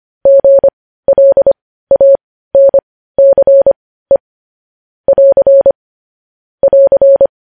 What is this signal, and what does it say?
Morse code, heard as a keyed tone, about 25 wpm, spaced as for 15 wpm, 550 Hz. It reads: GLANCE <AR> <AR>
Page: News Headlines in Morse Code at 15 WPM